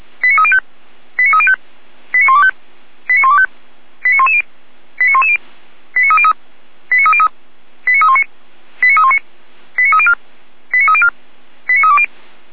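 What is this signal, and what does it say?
Five-tone selective call (ZVEI), heard as quick runs of five tones, tones of about 70 ms. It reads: 89305 89305 89125 89125 89110 89110 89303 89303 89319 89319 89304 89304 89320
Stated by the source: SDR-received ZVEI five-tone selective call, address 89276.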